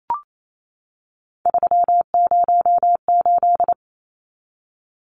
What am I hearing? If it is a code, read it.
Morse code, heard as a keyed tone, about 28 wpm, 700 Hz. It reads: 308
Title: Numbers - 3-Digit 28wpm